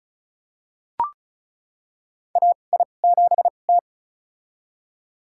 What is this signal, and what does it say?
Morse code, heard as a keyed tone, about 34 wpm, spaced as for 25 wpm, 700 Hz. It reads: AI7T
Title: USA Call Signs - 2x Character Spacing - 35wpm